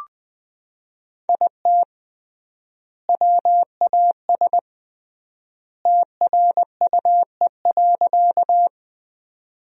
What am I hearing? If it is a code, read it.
Morse code, heard as a keyed tone, about 20 wpm, 700 Hz. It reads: IT WAS TRUE.
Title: 3x Word-Spacing - Sentences from Top 400 Words - 20wpm